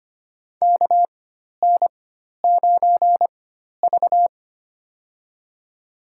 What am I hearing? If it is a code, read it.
Morse code, heard as a keyed tone, about 25 wpm, spaced as for 12 wpm, 700 Hz. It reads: KN9V
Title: USA Call Signs - 4 Characters - 4x Character Spacing - 25wpm